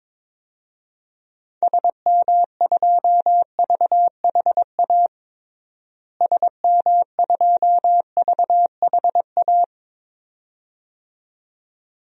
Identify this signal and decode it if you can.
Morse code, heard as a keyed tone, about 22 wpm, 700 Hz. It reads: SM2VHA SM2VHA
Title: DXCC-100 Call Signs - 6 Characters - Repeated 2x - 22wpm